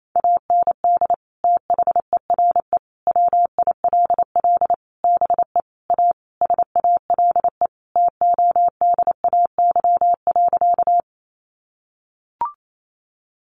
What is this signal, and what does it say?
Morse code, heard as a keyed tone, about 28 wpm, 700 Hz. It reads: AND THERE WILL BE A SALE TODAY.
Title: Sentences from Top 1400 Words 28wpm